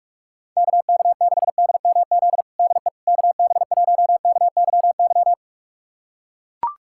Morse code, heard as a keyed tone, about 44 wpm, 700 Hz. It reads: KK6DMZ DE KB1KYY
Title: USA Callsign DE Callsign - 6 Characters - 45wpm